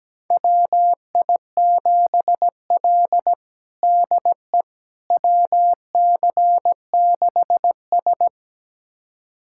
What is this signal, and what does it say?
Morse code, heard as a keyed tone, about 17 wpm, 700 Hz. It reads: WI7L DE WC6S
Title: USA Callsign DE Callsign - 4 Characters - 17wpm